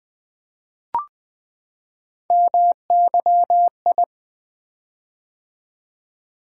Morse code, heard as a keyed tone, about 20 wpm, 700 Hz. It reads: MYI
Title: Three Letter-Number Combinations - 20wpm